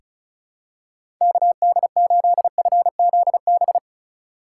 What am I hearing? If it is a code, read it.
Morse code, heard as a keyed tone, about 36 wpm, 700 Hz. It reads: KD8FZB
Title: USA Call Signs - 6x Character Spacing - 35wpm